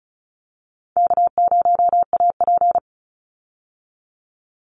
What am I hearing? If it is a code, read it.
Morse code, heard as a keyed tone, about 34 wpm, 700 Hz. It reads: K0AP
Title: Easy Call Signs Repeated 3x - 35wpm